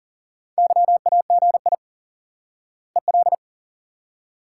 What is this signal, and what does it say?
Morse code, heard as a keyed tone, about 40 wpm, 700 Hz. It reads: YAGI EL
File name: IQE Course 3 - Lesson 89 - YAGI - Two Elements - 40wpm